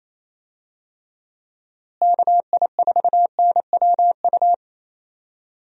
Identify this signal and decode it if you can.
Morse code, heard as a keyed tone, about 28 wpm, 700 Hz. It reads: KI4NWU
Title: USA Call Signs - 6 Characters - Repeated 3x - 28wpm